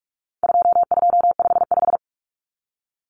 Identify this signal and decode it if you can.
Morse code, heard as a keyed tone, about 44 wpm, 700 Hz. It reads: 2255
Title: Numbers - 4-Digit 45wpm